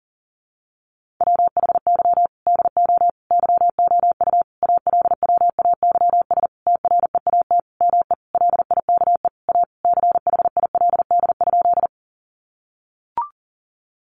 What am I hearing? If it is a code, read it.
Morse code, heard as a keyed tone, about 40 wpm, 700 Hz. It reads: WHY DO YOU ALWAYS TREAT ME LIKE A CHILD?
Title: Sentences from Top 1300 Words 40wpm